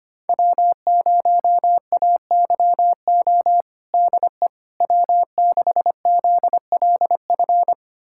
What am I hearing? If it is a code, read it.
Morse code, heard as a keyed tone, about 25 wpm, 700 Hz. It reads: W0AYO DE W6ZLF
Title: USA Callsign DE Callsign - 5 Characters - 25wpm